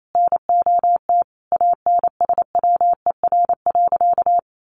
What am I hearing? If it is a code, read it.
Morse code, heard as a keyed tone, about 28 wpm, 700 Hz. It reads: NOT ANSWER.
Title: Sentences from Top 800 Words 28wpm